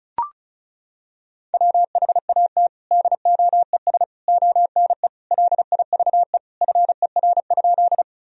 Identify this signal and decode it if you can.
Morse code, heard as a keyed tone, about 34 wpm, 700 Hz. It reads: WHAT DOES ONE LIVE FER?
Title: Sentences from Top 200 Words and Abbreviations 35wpm